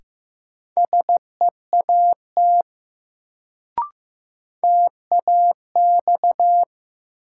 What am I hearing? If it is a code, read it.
Morse code, heard as a keyed tone, about 15 wpm, 700 Hz. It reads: SEAT TAX
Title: Sentences from Top 1500 Words - New Word Drill Alt - 15wpm